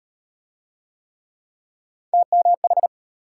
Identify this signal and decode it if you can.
Morse code, heard as a keyed tone, about 36 wpm, 700 Hz. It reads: TMH